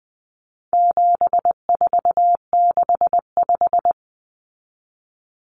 Morse code, heard as a keyed tone, about 20 wpm, 700 Hz. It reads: 7465
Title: Numbers - 4-Digit 20wpm